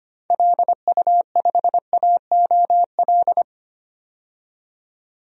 Morse code, heard as a keyed tone, about 25 wpm, 700 Hz. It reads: LU5AOL
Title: DXCC-100 Call Signs - 6 Characters - 25wpm